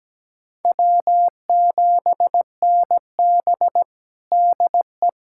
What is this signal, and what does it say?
Morse code, heard as a keyed tone, about 17 wpm, 700 Hz. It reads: W7NB DE
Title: USA Callsign DE Callsign - 4 Characters - 17wpm